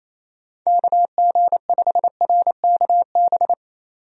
Morse code, heard as a keyed tone, about 27 wpm, 700 Hz. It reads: KG5RKB
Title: USA Call Signs - 6 Characters - 28wpm